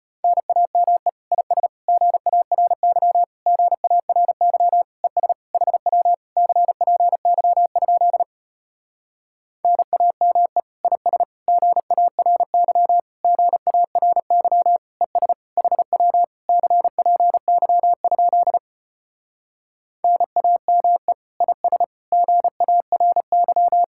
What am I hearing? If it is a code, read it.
Morse code, heard as a keyed tone, about 33 wpm, 690 Hz. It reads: NAME IS GARY GARY ES HW CPY? NAME IS GARY GARY ES HW CPY? NAME IS GARY